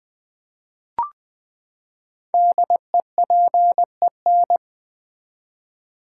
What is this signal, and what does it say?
Morse code, heard as a keyed tone, about 20 wpm, 700 Hz. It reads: DEPEN
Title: Build A Word - 5 Letters At A Time - 10-Letter Words - 20wpm